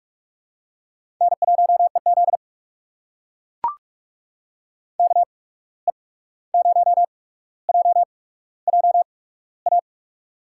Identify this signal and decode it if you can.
Morse code, heard as a keyed tone, about 42 wpm, spaced as for 17 wpm, 700 Hz. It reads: N1EZ KE0JJA
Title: USA Call Signs - 8x Character Spacing - 45wpm